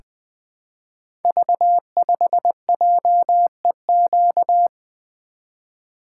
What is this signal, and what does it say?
Morse code, heard as a keyed tone, about 20 wpm, 700 Hz. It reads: V5JEQ
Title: Code Groups - 5 Letter:Numbers - 20wpm